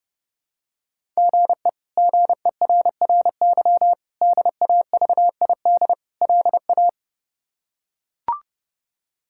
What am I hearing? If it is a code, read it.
Morse code, heard as a keyed tone, about 30 wpm, 700 Hz. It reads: GE GERRY DAVID LA
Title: K1USN SST QSO Practice - 30wpm